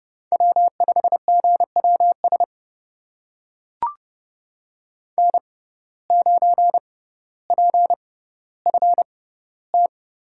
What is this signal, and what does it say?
Morse code, heard as a keyed tone, about 30 wpm, spaced as for 15 wpm, 700 Hz. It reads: W5GWS N9PFT
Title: USA Call Signs - 5 Characters - 6x Character Spacing - 30wpm